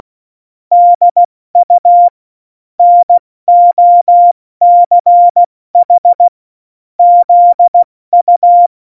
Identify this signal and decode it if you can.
Morse code, heard as a keyed tone, about 16 wpm, 700 Hz. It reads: DU NOCH ZU
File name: Morsekurs / Witze / Witz 12. September 17, Tempo 70